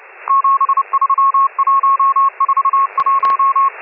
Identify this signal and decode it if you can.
Morse code, heard as a keyed tone, about 30 wpm, 1090 Hz, with radio noise. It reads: 73141